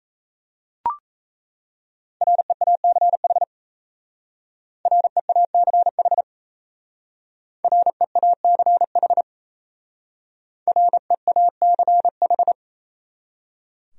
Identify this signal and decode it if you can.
Morse code, heard as a keyed tone, about 35 wpm, 700 Hz. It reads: REACH REACH REACH REACH
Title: Speed-Racing Top 500 Words 28wpm